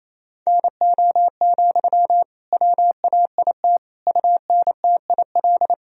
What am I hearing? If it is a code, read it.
Morse code, heard as a keyed tone, about 28 wpm, 700 Hz. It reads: NO, WAIT UNTIL
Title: Sentences from Top 1000 Words - Review Part 1 - 28wpm